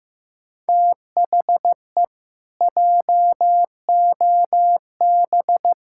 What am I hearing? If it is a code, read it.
Morse code, heard as a keyed tone, about 15 wpm, 700 Hz. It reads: THE JOB
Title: Pangram Challenge v2 - 15wpm